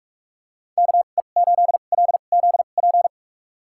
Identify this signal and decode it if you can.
Morse code, heard as a keyed tone, about 45 wpm, spaced as for 32 wpm, 700 Hz. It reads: KE8LZP
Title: USA Call Signs - 6 Characters - 2x Character Spacing - 45wpm